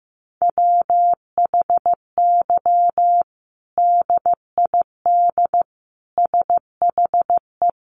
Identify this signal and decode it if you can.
Morse code, heard as a keyed tone, about 15 wpm, 700 Hz. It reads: WHY DID SHE
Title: Sentences from Top 1200 Words 15wpm